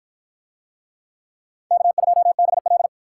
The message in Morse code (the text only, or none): K2BL